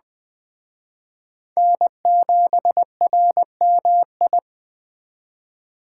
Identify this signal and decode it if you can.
Morse code, heard as a keyed tone, about 21 wpm, 700 Hz. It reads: N7RMI
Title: Speed Racing - USA 1x3 Call Signs 17wpm